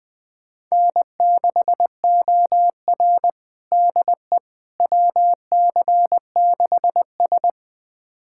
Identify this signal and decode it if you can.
Morse code, heard as a keyed tone, about 20 wpm, 700 Hz. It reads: N6OR DE WC6S
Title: USA Callsign DE Callsign - 4 Characters - 20wpm